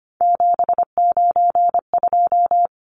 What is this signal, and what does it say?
Morse code, heard as a keyed tone, about 25 wpm, 700 Hz. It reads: 792
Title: Numbers - 3-Digit 25wpm